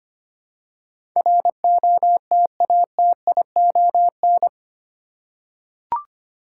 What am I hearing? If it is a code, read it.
Morse code, heard as a keyed tone, about 25 wpm, 700 Hz. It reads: ROTATION
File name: 8 Letter Words - First 3 Letters Given - 25wpm